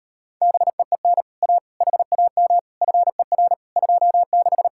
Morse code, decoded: BEEN A HAM FER 26

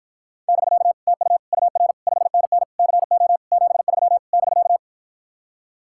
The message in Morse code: <BK> TU UR 5NN CO 73 <BK>

